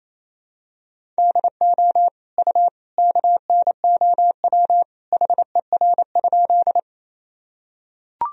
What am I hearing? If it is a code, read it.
Morse code, heard as a keyed tone, about 27 wpm, 700 Hz. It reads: DO U KNOW HER?
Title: Sentences from Top 100 Words and Abbreviations 28wpm